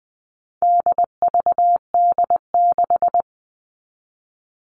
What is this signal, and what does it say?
Morse code, heard as a keyed tone, about 20 wpm, 700 Hz. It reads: DVD6
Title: All Cut Numbers - 4-Digit 20wpm